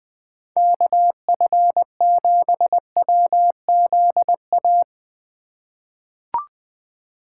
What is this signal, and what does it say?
Morse code, heard as a keyed tone, about 20 wpm, 700 Hz. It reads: KF7WZA
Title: USA Call Signs - 6 Characters - 2x Character Spacing - 20wpm